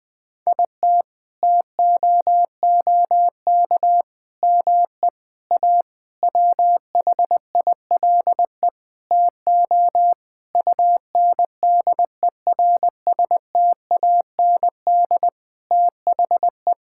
Morse code, heard as a keyed tone, about 20 wpm, 700 Hz. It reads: IT TOOK ME A WHILE TO UNDERSTAND THE
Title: Sentences with 6-Word Phrases - Spoken Beginning - 20wpm